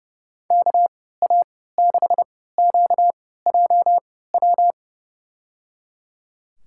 Morse code, heard as a keyed tone, about 29 wpm, spaced as for 17 wpm, 700 Hz. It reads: KA6QJW
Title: USA Call Signs - 3x Character Spacing - 30wpm